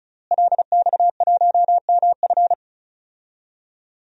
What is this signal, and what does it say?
Morse code, heard as a keyed tone, about 36 wpm, 700 Hz. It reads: LX1MF
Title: DXCC-100 Call Signs - 5 Characters - Repeated 2x - 35wpm